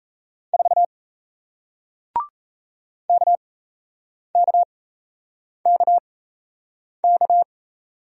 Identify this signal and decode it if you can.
Morse code, heard as a keyed tone, about 38 wpm, spaced as for 8 wpm, 700 Hz. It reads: 4 KKKK